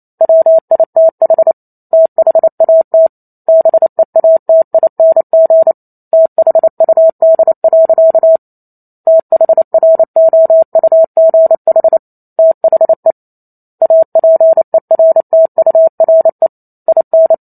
Morse code, heard as a keyed tone, about 28 wpm, 650 Hz. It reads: WITH THAT BEATING THUD. THROUGH THE APERTURE IN